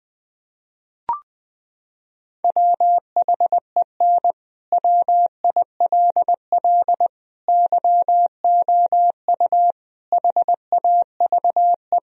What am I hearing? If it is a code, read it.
Morse code, heard as a keyed tone, about 20 wpm, 700 Hz. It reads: WHEN WILL YOU HAVE
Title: Sentences from Top 1000 Words - Review Part 3 - 20wpm